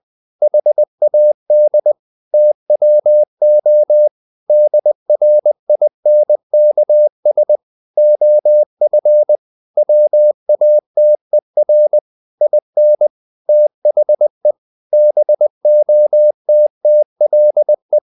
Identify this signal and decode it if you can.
Morse code, heard as a keyed tone, about 20 wpm, 600 Hz. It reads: HAD TWO DRINKS OF WATER IN THE BOTTLE
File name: the old man and the sea part 2 20wpm